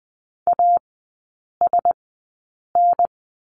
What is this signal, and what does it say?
Morse code, heard as a keyed tone, about 20 wpm, spaced as for 8 wpm, 700 Hz. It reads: ASN
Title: 041 Lesson 8e - Number 1 - Three Characters 20wpm